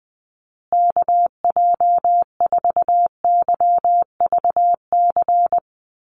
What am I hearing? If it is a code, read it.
Morse code, heard as a keyed tone, about 20 wpm, 700 Hz. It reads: KJ4YVC
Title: USA Call Signs Repeated 2x 20wpm